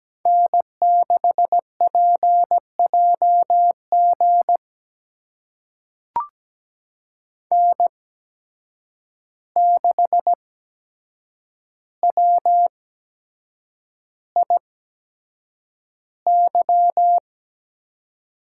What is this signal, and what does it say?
Morse code, heard as a keyed tone, about 17 wpm, spaced as for 7 wpm, 700 Hz. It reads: N6PJG N6WIY
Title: USA Call Signs - 5 Characters - 8x Character Spacing - 17wpm